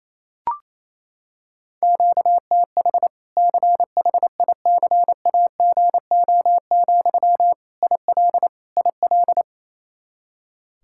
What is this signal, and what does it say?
Morse code, heard as a keyed tone, about 28 wpm, 700 Hz. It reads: QTH CHICAGO, IL IL
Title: Practice QSO - Exchange 1 - Beginning - 28wpm